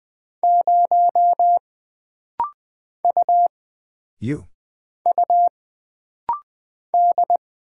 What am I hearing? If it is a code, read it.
Morse code, heard as a keyed tone, about 20 wpm, 700 Hz. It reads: 0 U U D
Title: Single Character (Letter:Number:Punctuation:Prosign) - Rapid-Fire - 20wpm